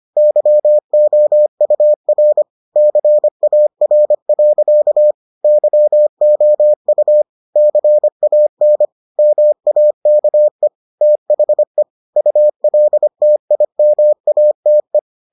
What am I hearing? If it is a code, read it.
Morse code, heard as a keyed tone, about 25 wpm, 600 Hz. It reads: YOUR CAR. YOU CAN MAKE THE ULTIMATE